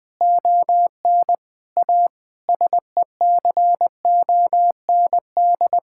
Morse code, heard as a keyed tone, about 20 wpm, 700 Hz. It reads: ON A SECOND
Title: Sentences from Top 1000 Words - Review Part 5 - 20wpm